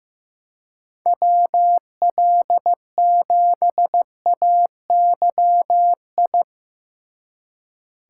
Morse code, heard as a keyed tone, about 17 wpm, 700 Hz. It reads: WL7AYI
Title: Speed Racing - USA Call Signs 15wpm